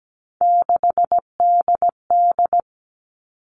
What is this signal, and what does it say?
Morse code, heard as a keyed tone, about 17 wpm, 700 Hz. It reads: 6DD
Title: All Cut Numbers - 3-Digit 17wpm